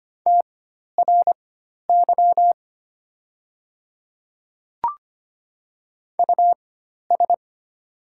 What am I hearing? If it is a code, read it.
Morse code, heard as a keyed tone, about 25 wpm, spaced as for 12 wpm, 700 Hz. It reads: TRY US
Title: 4x Character Spacing - Top 200 Words - 25wpm